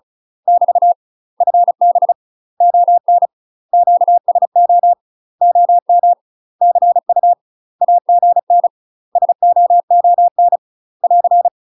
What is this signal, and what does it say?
Morse code, heard as a keyed tone, about 36 wpm, 700 Hz. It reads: <BT> FB ON QSO OM CU AGN SOON <AR>
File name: UK MASTER QSOs TRAINING / 35WPM / UK MASTER QSOs 0032